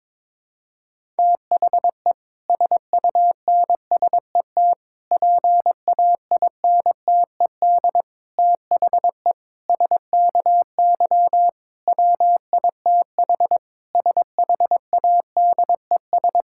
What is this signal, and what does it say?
Morse code, heard as a keyed tone, about 22 wpm, 700 Hz. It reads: THE SUNSET PAINTED THE SKY WITH SHADES
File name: Difficult Sentences with Hints - 22wpm